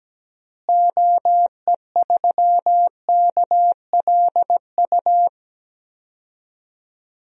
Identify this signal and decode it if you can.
Morse code, heard as a keyed tone, about 17 wpm, 700 Hz. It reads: OE3KLU
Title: DXCC-100 Call Signs - 6 Characters - Repeated 3x - 17wpm